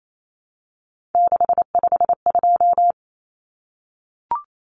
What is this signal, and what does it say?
Morse code, heard as a keyed tone, about 28 wpm, 700 Hz. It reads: 652
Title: Common Cut Numbers - 3-Digit 28wpm